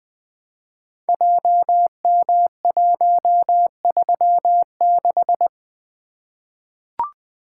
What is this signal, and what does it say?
Morse code, heard as a keyed tone, about 20 wpm, 700 Hz. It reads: JM136
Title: Code Groups - 5 Letter:Numbers - 20wpm